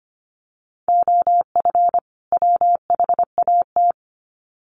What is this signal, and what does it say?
Morse code, heard as a keyed tone, about 25 wpm, 700 Hz. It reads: OF WHAT
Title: Sets of 2 Words 25wpm